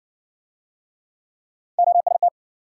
Morse code, heard as a keyed tone, about 60 wpm, 700 Hz. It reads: OST